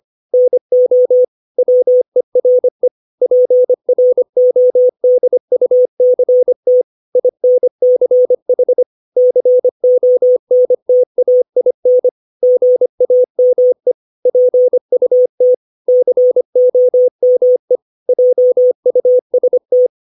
Morse code, heard as a keyed tone, about 25 wpm, 500 Hz. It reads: NO WERE PRODUCT INCH CONTAIN GAME PUT COME JUST
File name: 632 words / 632 Words 25WPM